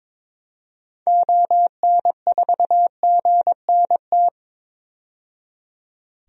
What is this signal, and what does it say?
Morse code, heard as a keyed tone, about 22 wpm, 700 Hz. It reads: ON4GNT
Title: DXCC-100 Call Signs - 6 Characters - 22wpm